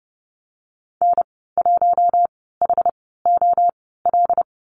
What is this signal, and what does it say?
Morse code, heard as a keyed tone, about 30 wpm, spaced as for 17 wpm, 700 Hz. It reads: N1HOL
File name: Lesson 15g - Number 5 - Call Signs 30wpm